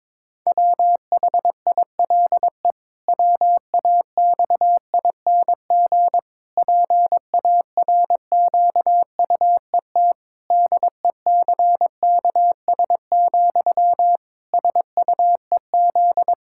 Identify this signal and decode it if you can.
Morse code, heard as a keyed tone, about 22 wpm, 700 Hz. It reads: WHILE WAXING PARQUET DECKS, SUEZ
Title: Pangram Challenge v2 - 22wpm